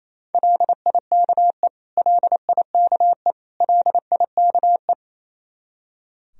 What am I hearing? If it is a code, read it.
Morse code, heard as a keyed tone, about 28 wpm, 700 Hz. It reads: LIKE LIKE LIKE
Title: Top 100 Words Repeated 28wpm